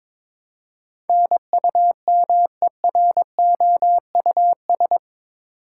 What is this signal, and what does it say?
Morse code, heard as a keyed tone, about 22 wpm, 700 Hz. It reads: NUMEROUS
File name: practice / 22wpm / Eight Letter Words - 22wpm